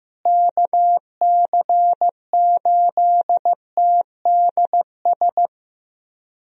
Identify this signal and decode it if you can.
Morse code, heard as a keyed tone, about 15 wpm, 700 Hz. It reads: KC8TDS
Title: USA Call Signs - 6 Characters - Repeated 3x - 15wpm